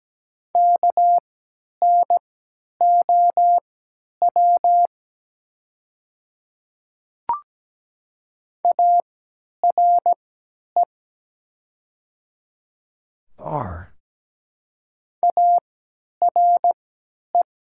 Morse code, heard as a keyed tone, about 17 wpm, spaced as for 10 wpm, 700 Hz. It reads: KNOW ARE ARE
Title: 3x Character Spacing - Top 200 Words - 17wpm